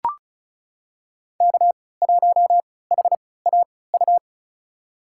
Morse code, heard as a keyed tone, about 36 wpm, spaced as for 20 wpm, 700 Hz. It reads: K1HAU
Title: USA Call Signs - 5 Characters - 3x Character Spacing - 35wpm